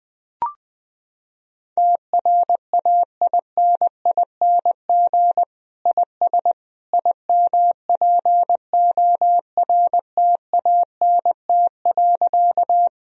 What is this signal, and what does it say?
Morse code, heard as a keyed tone, about 20 wpm, 700 Hz. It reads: TRAINING IS IMPORTANT.
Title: Sentences from Top 1000 Words - Review Part 4 - 20wpm